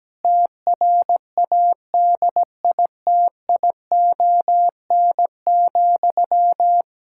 Morse code, heard as a keyed tone, about 17 wpm, 700 Hz. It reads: TRADITION,
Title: Pangram Challenge v2 - 17wpm